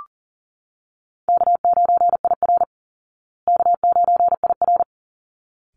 Morse code, heard as a keyed tone, about 40 wpm, 700 Hz. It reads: K9IR K9IR
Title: Call Signs Repeated 3x - 40wpm